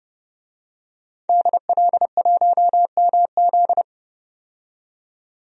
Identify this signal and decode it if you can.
Morse code, heard as a keyed tone, about 31 wpm, 700 Hz. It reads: DL1MZ
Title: DXCC-100 Call Signs - 5 Characters - 30wpm